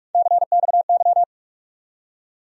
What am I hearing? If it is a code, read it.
Morse code, heard as a keyed tone, about 46 wpm, 700 Hz. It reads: CXY
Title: USA Call Sign Postfixes - 3 Letters 45wpm